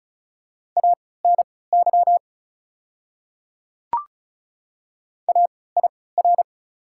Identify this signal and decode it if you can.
Morse code, heard as a keyed tone, about 36 wpm, spaced as for 20 wpm, 700 Hz. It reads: ANY AIR
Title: 3x Character Spacing - Top 200 Words - 35wpm